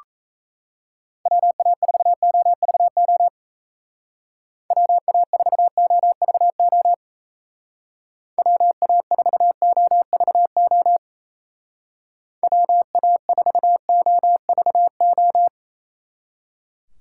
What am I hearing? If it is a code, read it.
Morse code, heard as a keyed tone, about 34 wpm, 700 Hz. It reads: WA4OVO WA4OVO WA4OVO WA4OVO
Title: Speed Racing - USA 2x3 Call Signs 28wpm